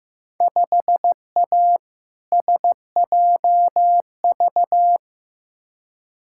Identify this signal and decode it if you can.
Morse code, heard as a keyed tone, about 15 wpm, 700 Hz. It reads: 5A SJV
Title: Field Day Exchanges - 15wpm